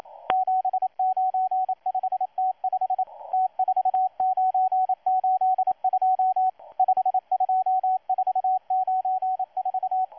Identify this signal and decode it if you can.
Morse code, heard as a keyed tone, about 27 wpm, 750 Hz, with radio noise. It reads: 795T5 T4982 52494
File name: M12-2016-07-07-1310utc-13926khz-error-restart